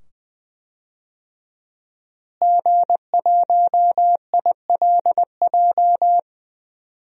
Morse code, heard as a keyed tone, about 20 wpm, 700 Hz. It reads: G1ILJ
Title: DXCC-100 Call Signs - 5 Characters - Repeated 2x - 20wpm